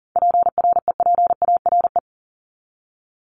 Morse code, heard as a keyed tone, about 40 wpm, 700 Hz. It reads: PREPARE
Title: Top 751-1000 Words 40wpm